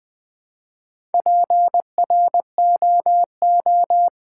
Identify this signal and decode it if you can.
Morse code, heard as a keyed tone, about 20 wpm, 700 Hz. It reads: PROO
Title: Build A Word - 1 Letter At A Time - 5-Letter Words - 20wpm